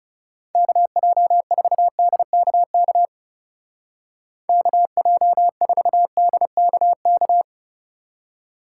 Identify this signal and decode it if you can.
Morse code, heard as a keyed tone, about 36 wpm, 700 Hz. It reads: KJ4DKK KJ4DKK
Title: Speed Racing - USA 2x3 Call Signs 30wpm